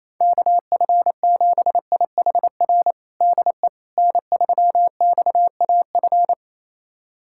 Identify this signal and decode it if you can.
Morse code, heard as a keyed tone, about 28 wpm, 700 Hz. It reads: KF7IHR DE N3XAF
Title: USA Callsign DE Callsign 28wpm